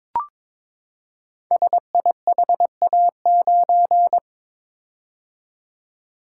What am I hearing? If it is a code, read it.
Morse code, heard as a keyed tone, about 22 wpm, 700 Hz. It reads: SIHA9
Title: Code Groups - 5 Letter:Numbers - 22wpm